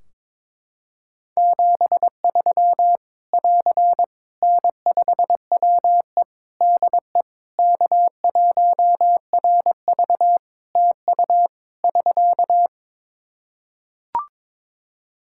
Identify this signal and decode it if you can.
Morse code, heard as a keyed tone, about 22 wpm, 700 Hz. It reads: 73 <AR> N5WE DE K1RV TU <SK>
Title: Practice QSO - Exchange 4 - Ending - 22wpm